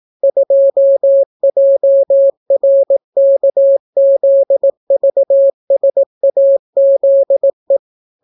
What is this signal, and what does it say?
Morse code, heard as a keyed tone, about 18 wpm, 550 Hz. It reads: 2JRKZVSAZE